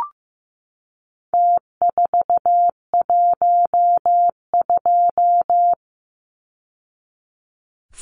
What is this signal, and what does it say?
Morse code, heard as a keyed tone, about 15 wpm, 700 Hz. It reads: T412
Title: Numbers - Contest Style - 4-Digit 15wpm